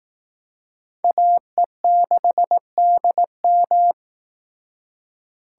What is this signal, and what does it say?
Morse code, heard as a keyed tone, about 18 wpm, 700 Hz. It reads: AE6DM
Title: Speed Racing - USA 2x2 Call Signs 15wpm